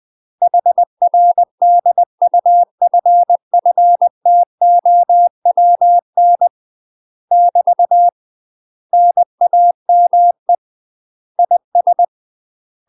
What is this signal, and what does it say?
Morse code, heard as a keyed tone, about 20 wpm, 700 Hz. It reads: HR DUFFTOWN <BT> NAME IS
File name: UK MASTER QSOs TRAINING / 20WPM / UK MASTER QSOs 0015